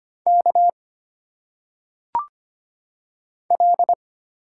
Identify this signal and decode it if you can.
Morse code, heard as a keyed tone, about 25 wpm, 700 Hz. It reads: K L
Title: Single Letters - Phonetic Pronunciation 25wpm